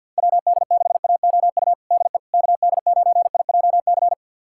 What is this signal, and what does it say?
Morse code, heard as a keyed tone, about 47 wpm, 700 Hz. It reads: WD6AOU DE KD9IJC
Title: USA Callsign DE Callsign - 6 Characters - 50wpm